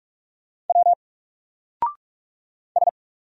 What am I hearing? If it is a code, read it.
Morse code, heard as a keyed tone, about 46 wpm, 700 Hz. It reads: W S
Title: Single Letter-Number 45wpm